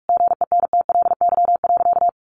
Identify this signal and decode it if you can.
Morse code, heard as a keyed tone, about 44 wpm, 700 Hz. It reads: GENTLY.
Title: Sentences from Top 1500 Words 45wpm